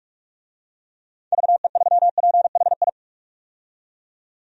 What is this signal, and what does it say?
Morse code, heard as a keyed tone, about 45 wpm, 700 Hz. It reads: VE3PHI
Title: DXCC-100 Call Signs - 6 Characters - 45wpm